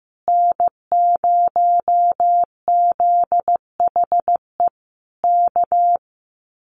Morse code, heard as a keyed tone, about 15 wpm, 700 Hz. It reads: N0ZHE K
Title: Common QSO Phrases Encore 15wpm